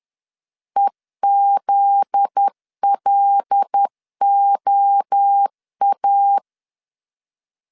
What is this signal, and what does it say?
Morse code, heard as a keyed tone, about 10 wpm, 800 Hz. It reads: EZLOA
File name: CW / DET / 1 9 1500C S A 10wpm 4511